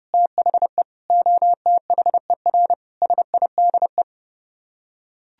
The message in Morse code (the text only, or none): THE OTHER SIDE